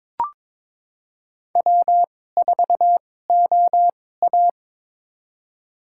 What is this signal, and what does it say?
Morse code, heard as a keyed tone, about 22 wpm, spaced as for 16 wpm, 700 Hz. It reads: W4OA
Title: USA Call Signs - 4 Characters - 2x Character Spacing - 22wpm